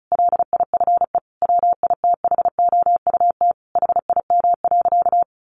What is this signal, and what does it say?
Morse code, heard as a keyed tone, about 34 wpm, 700 Hz. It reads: LIFE WITHOUT HIM.
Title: Sentences from Top 1400 Words 35wpm